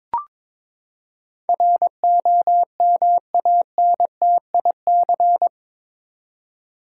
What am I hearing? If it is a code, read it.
Morse code, heard as a keyed tone, about 22 wpm, 700 Hz. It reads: ROMANTIC
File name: practice / 22wpm / Eight Letter Words - 22wpm